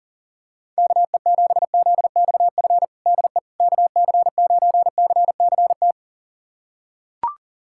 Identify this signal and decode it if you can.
Morse code, heard as a keyed tone, about 40 wpm, 700 Hz. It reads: KE7ZXF DE KC9CCT
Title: USA Callsign DE Callsign - 6 Characters - 40wpm